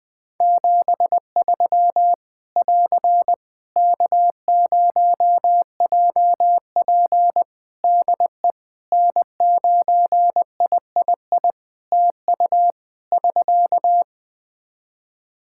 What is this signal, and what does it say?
Morse code, heard as a keyed tone, about 20 wpm, 700 Hz. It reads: 73 <AR> K0JP DE N9III TU <SK>
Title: Practice QSO - Exchange 4 - Ending - 20wpm